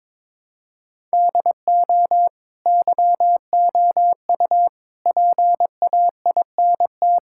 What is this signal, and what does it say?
Morse code, heard as a keyed tone, about 21 wpm, 700 Hz. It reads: DO YOU PAINT
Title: Sentences from Top 1000 Words - Review Part 6 - 22wpm